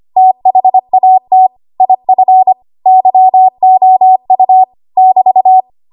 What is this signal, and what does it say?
Morse code, heard as a keyed tone, about 25 wpm, 750 Hz. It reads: THAT IF YOU <BT>
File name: Archive / 25 WPM / 25WPM